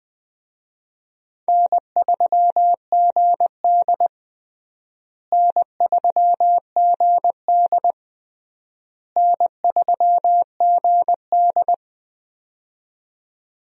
Morse code, heard as a keyed tone, about 20 wpm, 700 Hz. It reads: N3GD N3GD N3GD
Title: USA Call Signs - 4 Characters - Repeated 3x - 20wpm